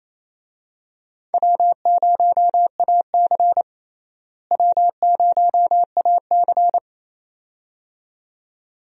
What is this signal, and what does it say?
Morse code, heard as a keyed tone, about 28 wpm, 700 Hz. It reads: W0AC W0AC
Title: USA Call Signs - 4 Characters - Repeated 2x - 28wpm